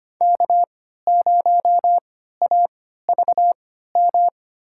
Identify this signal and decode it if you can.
Morse code, heard as a keyed tone, about 25 wpm, spaced as for 14 wpm, 700 Hz. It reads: K0AVM
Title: USA Call Signs - 5 Characters - 3x Character Spacing - 25wpm